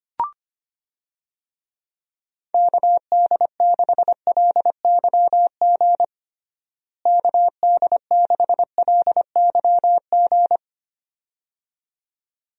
Morse code, heard as a keyed tone, about 25 wpm, 700 Hz. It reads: KD6LYG KD6LYG
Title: USA Call Signs - 6 Characters - Repeated 2x - 25wpm